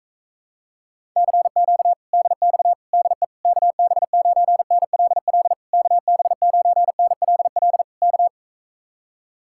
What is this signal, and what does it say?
Morse code, heard as a keyed tone, about 44 wpm, 700 Hz. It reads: CQ DX DE KB9NLL KB9NLL K